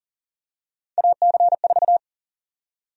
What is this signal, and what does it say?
Morse code, heard as a keyed tone, about 40 wpm, 700 Hz. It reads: AC4
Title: USA Call Sign Prefixes - 40wpm